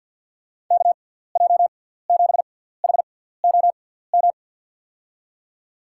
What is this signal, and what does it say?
Morse code, heard as a keyed tone, about 47 wpm, spaced as for 17 wpm, 700 Hz. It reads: KJ7HOM
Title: USA Call Signs - 6x Character Spacing - 50wpm